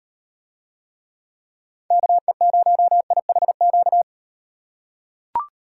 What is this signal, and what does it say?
Morse code, heard as a keyed tone, about 35 wpm, 700 Hz. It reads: KE0IHQ